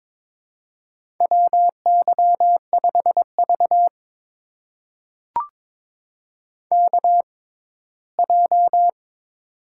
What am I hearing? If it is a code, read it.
Morse code, heard as a keyed tone, about 23 wpm, spaced as for 15 wpm, 700 Hz. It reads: WY5V KJ
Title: USA Call Signs - 6x Character Spacing - 22wpm